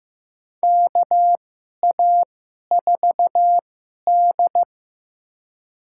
Morse code, heard as a keyed tone, about 15 wpm, spaced as for 11 wpm, 700 Hz. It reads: KA4D
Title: USA Call Signs - 4 Characters - 2x Character Spacing - 15wpm